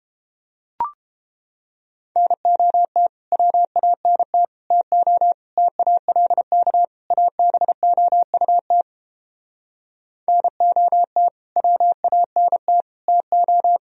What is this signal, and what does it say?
Morse code, heard as a keyed tone, about 29 wpm, 700 Hz. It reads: NOT WANT TO TALK ABOUT NOT WANT TO